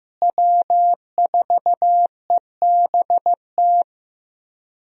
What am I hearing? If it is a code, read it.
Morse code, heard as a keyed tone, about 15 wpm, 700 Hz. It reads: W4EBT
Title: USA Call Signs - 5 Characters - Repeated 3x - 15wpm